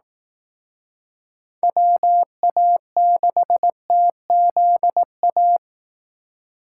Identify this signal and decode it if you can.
Morse code, heard as a keyed tone, about 18 wpm, 700 Hz. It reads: WA6TZA
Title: Speed Racing - USA Call Signs 15wpm